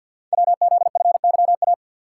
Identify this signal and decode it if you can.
Morse code, heard as a keyed tone, about 50 wpm, 700 Hz. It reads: WZFYA
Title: Code Groups - 5 Letters - 50wpm